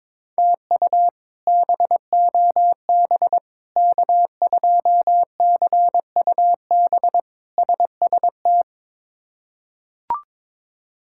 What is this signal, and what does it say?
Morse code, heard as a keyed tone, about 21 wpm, 700 Hz. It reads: TU BOB K2CUB SST
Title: K1USN SST QSO Practice - 22wpm